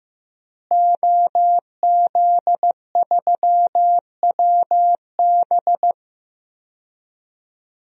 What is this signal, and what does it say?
Morse code, heard as a keyed tone, about 15 wpm, 700 Hz. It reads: OZ3WB
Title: DXCC-100 Call Signs - 5 Characters - Repeated 3x - 15wpm